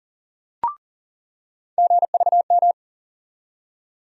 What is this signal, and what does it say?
Morse code, heard as a keyed tone, about 40 wpm, 700 Hz. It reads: GVM
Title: USA Call Sign Postfixes - 3 Letters 40wpm